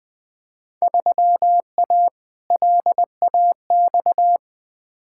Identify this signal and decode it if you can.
Morse code, heard as a keyed tone, about 20 wpm, 700 Hz. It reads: 3A LAX
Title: Field Day Exchanges - 20wpm